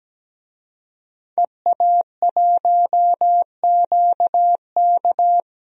Morse code, heard as a keyed tone, about 17 wpm, 700 Hz. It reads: EA1QK
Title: DXCC-100 Call Signs - 5 Characters - Repeated 3x - 17wpm